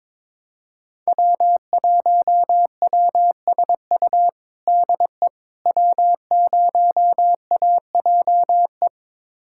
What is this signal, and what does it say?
Morse code, heard as a keyed tone, about 22 wpm, 700 Hz. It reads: W1WSU DE W0AJE
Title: USA Callsign DE Callsign - 5 Characters - 22wpm